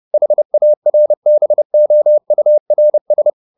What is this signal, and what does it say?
Morse code, heard as a keyed tone, about 30 wpm, 600 Hz. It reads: HARBOURS
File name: the old man and the sea / the old man and the sea 1 30wpm